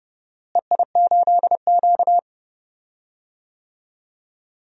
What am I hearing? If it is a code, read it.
Morse code, heard as a keyed tone, about 30 wpm, 700 Hz. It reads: EI8Q
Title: DXCC-100 Call Signs - 4 Characters - Repeated 2x - 30wpm